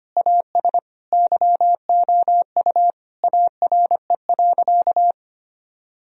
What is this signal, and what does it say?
Morse code, heard as a keyed tone, about 25 wpm, 700 Hz. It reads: AS YOU ARE.
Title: Sentences from Top 1000 Words - Review Part 4 - 25wpm